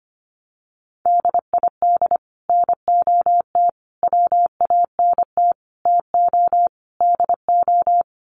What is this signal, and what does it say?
Morse code, heard as a keyed tone, about 25 wpm, 700 Hz. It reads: DID NOT WANT TO DO
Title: Sets of 5 Words Encore 25wpm